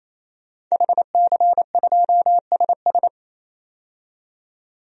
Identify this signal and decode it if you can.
Morse code, heard as a keyed tone, about 28 wpm, 700 Hz. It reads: HC2SS
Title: DXCC-100 Call Signs - 5 Characters - 28wpm